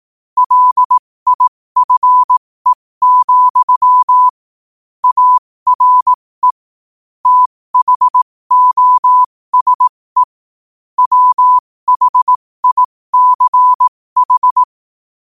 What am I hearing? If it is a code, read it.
Morse code, heard as a keyed tone, about 18 wpm, spaced as for 15 wpm, 1000 Hz. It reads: LIFE, ARE THOSE WHICH
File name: qotdfiles / 15wpm